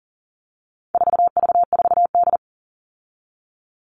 Morse code, heard as a keyed tone, about 40 wpm, 700 Hz. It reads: 4V4D
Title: All Cut Numbers - 4-Digit 40wpm